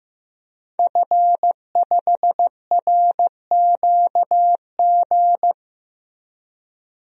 Morse code, heard as a keyed tone, about 15 wpm, 700 Hz. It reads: F5RQG
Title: DXCC-100 Call Signs - 5 Characters - Repeated 3x - 15wpm